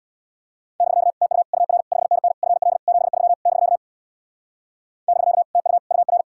Morse code, heard as a keyed tone, about 67 wpm, 700 Hz. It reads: <BK> TU UR 5NN FL 73 <BK> <BK> TU UR